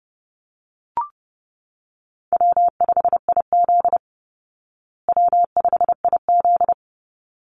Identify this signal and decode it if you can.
Morse code, heard as a keyed tone, about 30 wpm, 700 Hz. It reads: W5IZ W5IZ
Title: Call Signs Repeated 3x - 30wpm